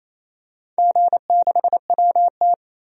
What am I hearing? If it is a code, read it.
Morse code, heard as a keyed tone, about 28 wpm, 700 Hz. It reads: G6WT
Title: DXCC-100 Call Signs - 4 Characters - 28wpm